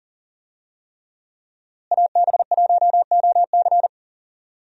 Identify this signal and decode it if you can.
Morse code, heard as a keyed tone, about 40 wpm, 700 Hz. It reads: AB1OC
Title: USA Call Signs - 5 Characters - Repeated 3x - 40wpm